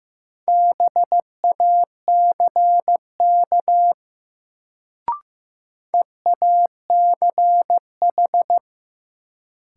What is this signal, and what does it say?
Morse code, heard as a keyed tone, about 15 wpm, 700 Hz. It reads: BACK EACH
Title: Sentences from Top 200 Words - New Word Drill - 15wpm